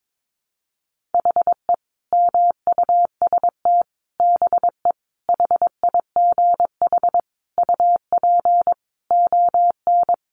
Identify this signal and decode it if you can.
Morse code, heard as a keyed tone, about 22 wpm, 700 Hz. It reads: HE MUST BE HIGH UP ON
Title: Sentences from Top 1200 Words 22wpm